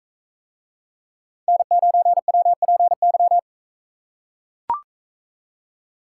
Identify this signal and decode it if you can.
Morse code, heard as a keyed tone, about 42 wpm, 700 Hz. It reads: N9WPY